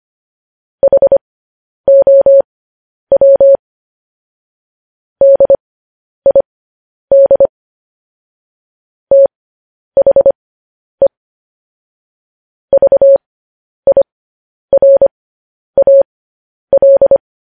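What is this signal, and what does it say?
Morse code, heard as a keyed tone, about 25 wpm, spaced as for 10 wpm, 550 Hz. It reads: HOW DID THE VIRAL
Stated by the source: Morse code transcription of news headlines sent at 10 wpm.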